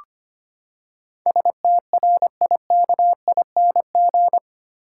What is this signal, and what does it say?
Morse code, heard as a keyed tone, about 25 wpm, 700 Hz. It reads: STRIKING
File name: Eight Letter Words - 25wpm